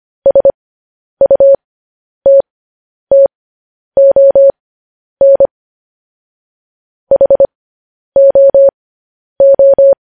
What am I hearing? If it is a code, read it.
Morse code, heard as a keyed tone, about 25 wpm, spaced as for 10 wpm, 550 Hz. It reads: SUTTON HOO
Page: news-headlines-in-morse-code-at-10-wpm